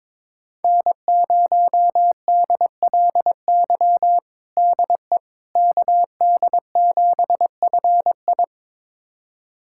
Morse code, heard as a keyed tone, about 21 wpm, 700 Hz. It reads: N0DLY DE KD7FI
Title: USA Callsign DE Callsign - 5 Characters - 22wpm